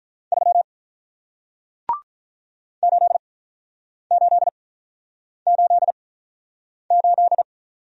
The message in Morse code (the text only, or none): 3 8 8 8 8